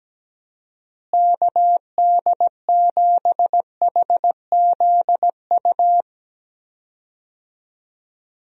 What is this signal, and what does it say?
Morse code, heard as a keyed tone, about 17 wpm, 700 Hz. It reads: KD7HZU
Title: USA Call Signs - 6 Characters - Repeated 3x - 17wpm